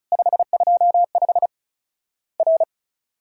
Morse code, heard as a keyed tone, about 36 wpm, 700 Hz. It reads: 525 R
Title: Worked All Europe DX Contest - QTC Exchange - 35wpm